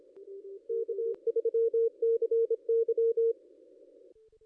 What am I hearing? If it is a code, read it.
Morse code, heard as a keyed tone, about 27 wpm, 470 Hz. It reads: K3CY
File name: contests / ss-cw-k5tr-op-n5rz -0500